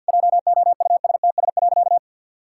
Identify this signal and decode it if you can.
Morse code, heard as a keyed tone, about 51 wpm, 700 Hz. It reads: JOUSTS.
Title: Pangram Challenge v2 - 50wpm